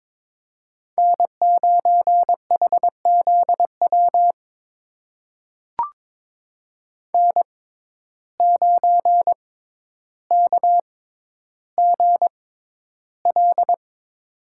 Morse code, heard as a keyed tone, about 22 wpm, spaced as for 11 wpm, 700 Hz. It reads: N9HZW N9KGL
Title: USA Call Signs - 5 Characters - 6x Character Spacing - 22wpm